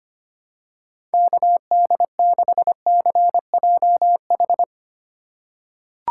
Morse code, heard as a keyed tone, about 25 wpm, 700 Hz. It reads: KD6CJH
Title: USA Call Signs - 6 Characters - 2x Character Spacing - 25wpm